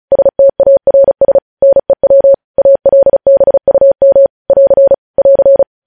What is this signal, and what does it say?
Morse code, heard as a keyed tone, about 36 wpm, 550 Hz. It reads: STARS NEW ALBUM <AR> <AR>
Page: News headlines in Morse code at 35 wpm